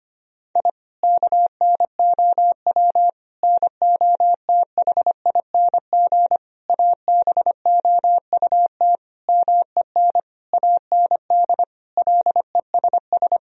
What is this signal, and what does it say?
Morse code, heard as a keyed tone, about 25 wpm, 700 Hz. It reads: I KNOW NOTHING ABOUT MEN AND LESS
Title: Sentences from Top 1000 Words - Review Part 2 - 25wpm